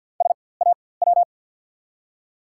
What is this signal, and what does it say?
Morse code, heard as a keyed tone, about 48 wpm, spaced as for 23 wpm, 700 Hz. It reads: SAW
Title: 4x Character Spacing - Top 500 Words - 50wpm